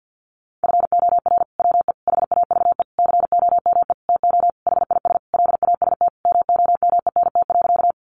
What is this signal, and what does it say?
Morse code, heard as a keyed tone, about 50 wpm, 700 Hz. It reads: FOR WE HAVE COME TO HIS LAST MOMENT.
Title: Sentences from Top 1000 Words 50wpm